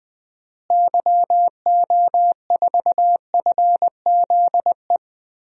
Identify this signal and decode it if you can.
Morse code, heard as a keyed tone, about 20 wpm, 700 Hz. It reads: YO4FZE
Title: DXCC-100 Call Signs - 6 Characters - Repeated 2x - 20wpm